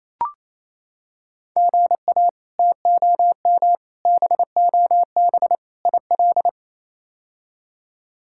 Morse code, heard as a keyed tone, about 28 wpm, 700 Hz. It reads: GA TOM BOB IL
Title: K1USN SST QSO Practice - 28wpm